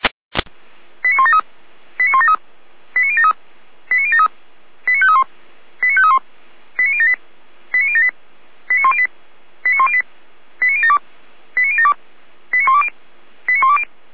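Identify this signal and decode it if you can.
Five-tone selective call (ZVEI), heard as quick runs of five tones, tones of about 70 ms. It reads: 89173 89173 89073 89073 89531 89531 89078 89078 89108 89108 89072 89072 89120 89120
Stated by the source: ZVEI five-tone selective call 89465 received on SDR.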